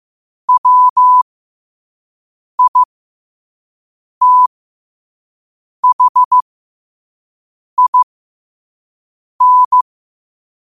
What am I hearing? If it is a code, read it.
Morse code, heard as a keyed tone, about 15 wpm, spaced as for 5 wpm, 1000 Hz. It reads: WITHIN
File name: qotdfiles / 5wpm